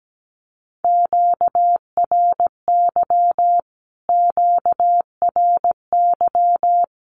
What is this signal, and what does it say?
Morse code, heard as a keyed tone, about 17 wpm, 700 Hz. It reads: QRY QRY
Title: Q-Codes Repeated 3x 17wpm